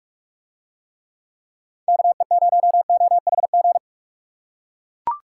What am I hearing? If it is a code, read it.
Morse code, heard as a keyed tone, about 42 wpm, 700 Hz. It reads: KE0OHG